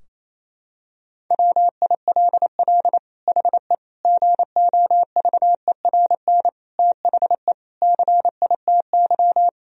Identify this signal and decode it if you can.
Morse code, heard as a keyed tone, about 28 wpm, 700 Hz. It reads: WILL HE GOVERN THE CITY
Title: Sentences from Top 1000 Words - Review Part 2 - 28wpm